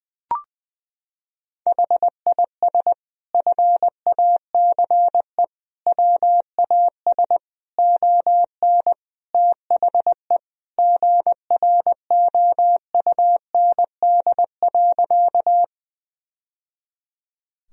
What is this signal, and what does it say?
Morse code, heard as a keyed tone, about 20 wpm, 700 Hz. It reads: HIS FACE WAS ON THE GROUND.
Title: Sentences from Top 1000 Words - Review Part 2 - 20wpm